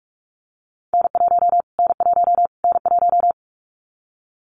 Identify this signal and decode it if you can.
Morse code, heard as a keyed tone, about 44 wpm, 700 Hz. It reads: N1 N1 N1
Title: Callsign Prefixes for Top 25 DXCC - Repeated 3x - 45wpm